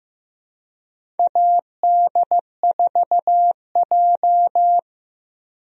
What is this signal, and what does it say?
Morse code, heard as a keyed tone, about 15 wpm, 700 Hz. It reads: AD4J
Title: USA Call Signs - 4 Characters - Repeated 2x - 15wpm